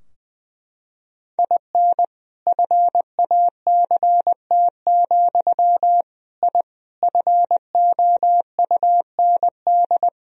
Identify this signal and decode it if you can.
Morse code, heard as a keyed tone, about 21 wpm, 700 Hz. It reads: IN FACT, I FOUND
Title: Sentences from Top 1000 Words - Review Part 3 - 20wpm